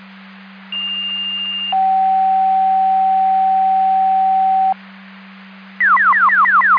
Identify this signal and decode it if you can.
Two-tone page, heard as a one-second tone, then a three-second tone, about 2800 then 760 Hz. It is heard from 0.7 to 4.7 s.